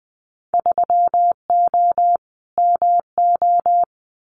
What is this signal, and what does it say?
Morse code, heard as a keyed tone, about 20 wpm, 700 Hz. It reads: 3O MO
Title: Winter Field Day Exchanges Encore 20wpm